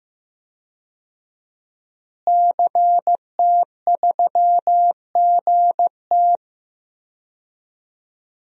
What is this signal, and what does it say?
Morse code, heard as a keyed tone, about 16 wpm, 700 Hz. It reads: CT3GT
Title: DXCC-100 Call Signs - 5 Characters - Repeated 2x - 15wpm